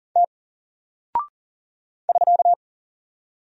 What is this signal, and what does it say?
Morse code, heard as a keyed tone, about 38 wpm, 700 Hz. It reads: T <SK>
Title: Single Character (Letter:Number:Punctuation:Prosign) 40wpm